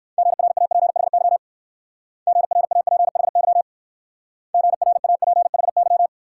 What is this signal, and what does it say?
Morse code, heard as a keyed tone, about 57 wpm, 690 Hz. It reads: GRAPHY GRAPHY GRAPHY